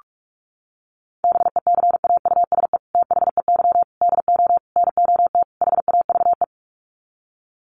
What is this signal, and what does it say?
Morse code, heard as a keyed tone, about 44 wpm, 690 Hz. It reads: BECAUSE THEY DO NOT HAVE
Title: Sets of 5 Words Encore2 45wpm